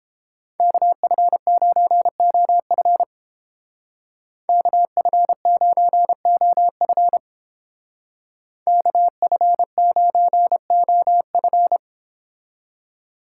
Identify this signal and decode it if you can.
Morse code, heard as a keyed tone, about 30 wpm, 700 Hz. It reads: KF9OF KF9OF KF9OF